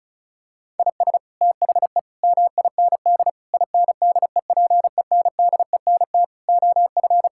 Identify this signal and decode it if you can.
Morse code, heard as a keyed tone, about 36 wpm, 700 Hz. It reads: IS THE MIND INDEPENDENT OF
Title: Sentences from Top 1000 Words - Review Part 6 - 35wpm